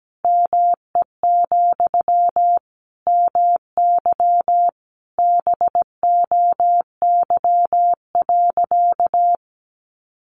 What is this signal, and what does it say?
Morse code, heard as a keyed tone, about 17 wpm, 700 Hz. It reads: ME, MY BOY.
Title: Sentences from Top 900 Words 17wpm